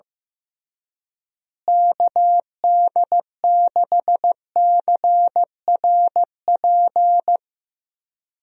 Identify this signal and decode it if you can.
Morse code, heard as a keyed tone, about 15 wpm, 700 Hz. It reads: KD6CRP
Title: USA Call Signs - 6 Characters - Repeated 3x - 15wpm